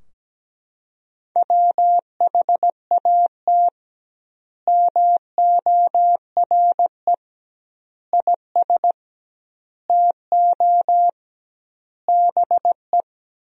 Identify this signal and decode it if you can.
Morse code, heard as a keyed tone, about 17 wpm, 700 Hz. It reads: WHAT MORE IS TO BE
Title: 2x Word-Spacing - Sentences from Top 500 Words - 17wpm